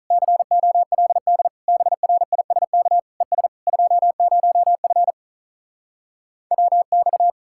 Morse code, heard as a keyed tone, about 44 wpm, 700 Hz. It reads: COLD BRISK ES 20F WX
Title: Speed-Racing Common QSO Phrases - 35wpm